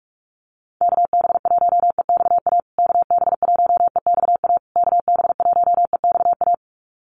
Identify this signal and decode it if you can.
Morse code, heard as a keyed tone, about 44 wpm, 700 Hz. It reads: KB1EXA KB1EXA KB1EXA
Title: USA Call Signs Repeated 3x 45wpm